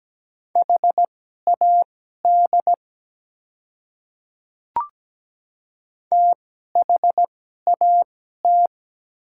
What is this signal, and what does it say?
Morse code, heard as a keyed tone, about 17 wpm, spaced as for 12 wpm, 700 Hz. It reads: HAD THAT
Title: 2x Character Spacing - Top 200 Words - 17wpm